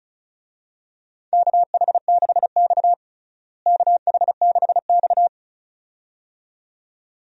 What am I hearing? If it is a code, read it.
Morse code, heard as a keyed tone, about 34 wpm, 700 Hz. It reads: KH6X KH6X
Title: DXCC-100 Call Signs - 4 Characters - Repeated 2x - 35wpm